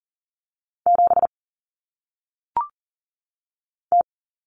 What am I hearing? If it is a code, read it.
Morse code, heard as a keyed tone, about 40 wpm, 700 Hz. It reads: 7 T
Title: Common Cut Numbers - 1-Digit 40wpm